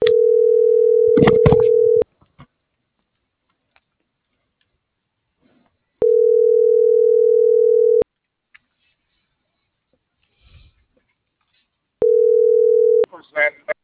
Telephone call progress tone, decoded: ringback tone

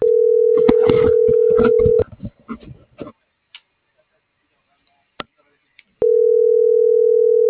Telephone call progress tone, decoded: ringback tone

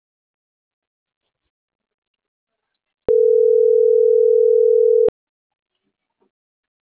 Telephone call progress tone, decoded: ringback tone

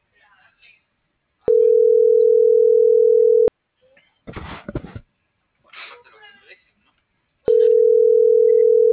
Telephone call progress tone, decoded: ringback tone

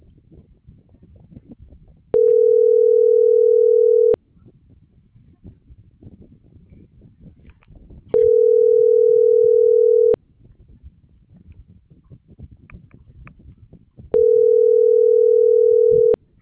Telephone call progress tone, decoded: ringback tone

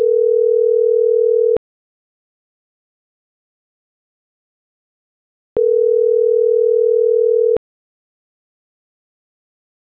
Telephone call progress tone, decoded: ringback tone

